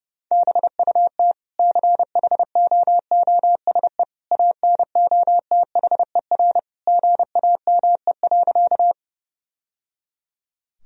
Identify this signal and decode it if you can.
Morse code, heard as a keyed tone, about 29 wpm, 700 Hz. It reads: BUT CHOOSE ANOTHER GAME.
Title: Sentences from Top 1000 Words - Review Part 4 - 30wpm